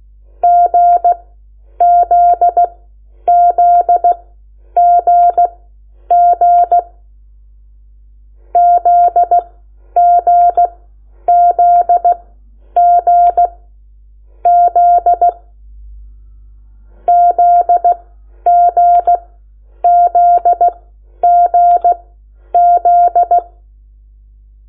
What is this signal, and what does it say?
Morse code, heard as a keyed tone, about 16 wpm, spaced as for 9 wpm, 700 Hz. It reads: GZZGG ZGZGZ ZGZGZ
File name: Lecciones / CLASE Nº8